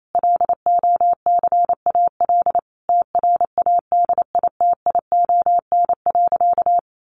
Morse code, heard as a keyed tone, about 28 wpm, 700 Hz. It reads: LOCAL TRADITION.
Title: Sentences from Top 1500 Words 28wpm